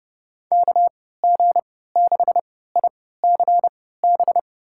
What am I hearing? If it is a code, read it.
Morse code, heard as a keyed tone, about 32 wpm, spaced as for 17 wpm, 700 Hz. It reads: KG6ICB
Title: USA Call Signs - 6 Characters - 3x Character Spacing - 30wpm